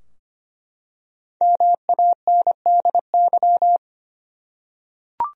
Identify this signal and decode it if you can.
Morse code, heard as a keyed tone, about 25 wpm, 700 Hz. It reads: MANDY
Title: USA Common Names - 25wpm